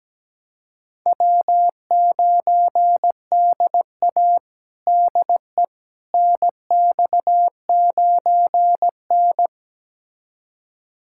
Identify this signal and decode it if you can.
Morse code, heard as a keyed tone, about 17 wpm, 700 Hz. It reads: W9DA DE NX9N
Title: USA Callsign DE Callsign - 4 Characters - 17wpm